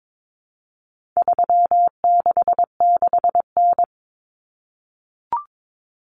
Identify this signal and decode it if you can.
Morse code, heard as a keyed tone, about 22 wpm, 700 Hz. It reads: 366N
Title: Common Cut Numbers - 4-Digit 22wpm